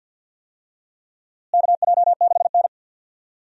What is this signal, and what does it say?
Morse code, heard as a keyed tone, about 51 wpm, 700 Hz. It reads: KJ6N
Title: USA Call Signs - 4 Characters - Repeated 3x - 50wpm